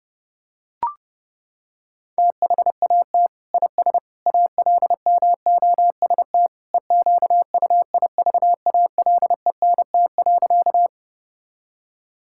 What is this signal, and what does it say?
Morse code, heard as a keyed tone, about 32 wpm, 700 Hz. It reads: THAT IS ALMOST EQUIVALENT.
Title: Sentences from Top 1600 Words 30wpm